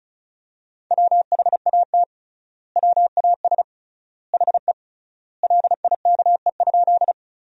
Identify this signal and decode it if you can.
Morse code, heard as a keyed tone, about 36 wpm, 700 Hz. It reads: WHAT WAS HE LIKE?
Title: 3x Word-Spacing - Sentences from Top 100 Words - 35wpm